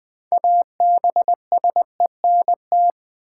Morse code, heard as a keyed tone, about 20 wpm, 700 Hz. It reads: ABSENT
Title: Build A Word - 3 Letters At A Time - 6-Letter Words - 20wpm